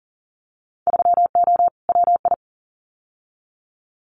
Morse code, heard as a keyed tone, about 40 wpm, 700 Hz. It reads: 3O WI
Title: Winter Field Day Exchanges 40wpm